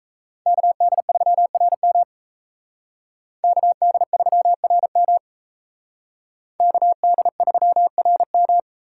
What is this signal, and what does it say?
Morse code, heard as a keyed tone, about 36 wpm, 700 Hz. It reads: KD3RM KD3RM KD3RM